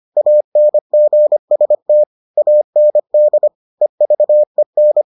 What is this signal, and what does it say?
Morse code, heard as a keyed tone, about 25 wpm, 600 Hz. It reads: ANGST AND EVEN